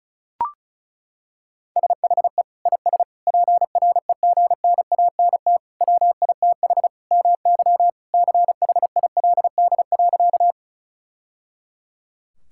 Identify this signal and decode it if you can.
Morse code, heard as a keyed tone, about 36 wpm, 700 Hz. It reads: SHE IS PREGNANT WITH MY CHILD.
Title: Sentences from Top 1000 Words - Review Part 6 - 35wpm